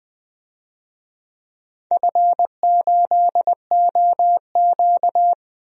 Morse code, heard as a keyed tone, about 20 wpm, 700 Hz. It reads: F8OQ
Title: DXCC-100 Call Signs - 4 Characters - 20wpm